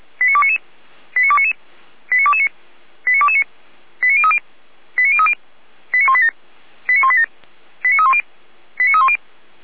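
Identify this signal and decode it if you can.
Five-tone selective call (ZVEI), heard as quick runs of five tones, tones of about 70 ms. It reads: 89299 89299 89229 89229 89030 89030 89187 89187 89310 89310
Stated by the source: ZVEI five-tone call received via SDR, logged by Feuerwehr Orlamünde.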